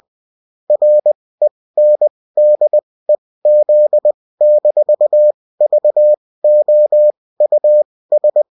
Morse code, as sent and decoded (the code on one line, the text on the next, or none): .-. -. -.. --.. -....- ...- --- ..- ...
RENDEZ-VOUS